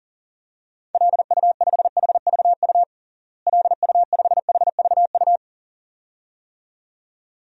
Morse code, heard as a keyed tone, about 40 wpm, 700 Hz. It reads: LU5HVU LU5HVU
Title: DXCC-100 Call Signs - 6 Characters - Repeated 2x - 40wpm